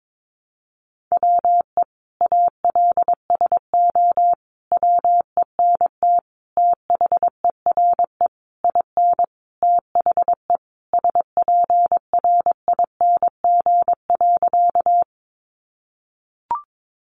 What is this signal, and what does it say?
Morse code, heard as a keyed tone, about 22 wpm, 700 Hz. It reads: WE ALSO WENT THERE IN THE SPRING.
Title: Sentences from Top 1200 Words 22wpm